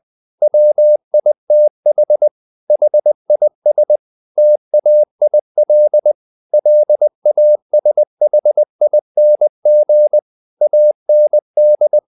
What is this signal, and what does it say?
Morse code, heard as a keyed tone, about 20 wpm, 600 Hz. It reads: WITH HIS TAIL LASHING AND
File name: the old man and the sea part 2 20wpm